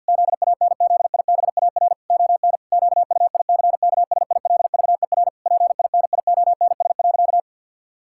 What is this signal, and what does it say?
Morse code, heard as a keyed tone, about 47 wpm, 700 Hz. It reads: ZANZIBAR ON QUICKSILVER PINIONS.
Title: Pangram Challenge v2 - 50wpm